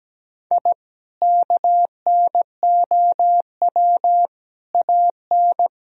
Morse code, heard as a keyed tone, about 17 wpm, 700 Hz. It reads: I KNOW AN
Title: Sentences from Top 1000 Words - Review Part 3 - 17wpm